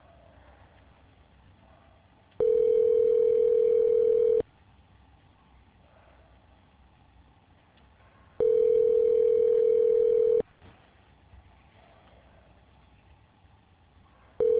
Ringback tone (North America), 440 + 480 Hz.